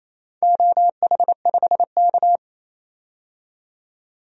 Morse code, heard as a keyed tone, about 28 wpm, 700 Hz. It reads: OH5K
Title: DXCC-100 Call Signs - 4 Characters - 28wpm